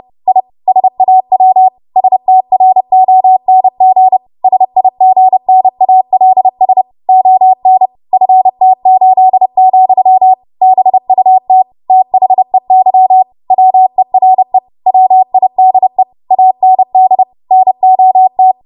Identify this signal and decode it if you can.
Morse code, heard as a keyed tone, about 30 wpm, 750 Hz. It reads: I SAW STRONG SIGNALS ON FT8, BUT THEY WERE WIDE AND NOT
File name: Morse / 30WPM